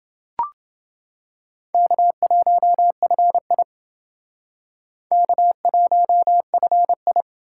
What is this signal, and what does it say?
Morse code, heard as a keyed tone, about 29 wpm, 700 Hz. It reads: K1FI K1FI